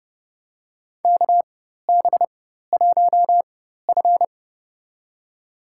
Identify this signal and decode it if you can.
Morse code, heard as a keyed tone, about 29 wpm, spaced as for 14 wpm, 700 Hz. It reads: KB1F
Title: USA Call Signs - 4 Characters - 4x Character Spacing - 30wpm